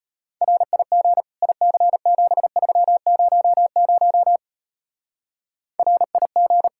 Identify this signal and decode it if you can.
Morse code, heard as a keyed tone, about 36 wpm, 700 Hz. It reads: RIG IC7300 RIG